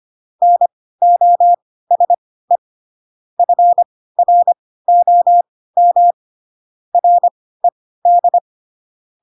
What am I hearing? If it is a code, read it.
Morse code, heard as a keyed tone, about 25 wpm, spaced as for 16 wpm, 700 Hz. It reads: NOSE FROM RED
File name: Pinocchio / Pinocchio ch1 10 25 ik1hge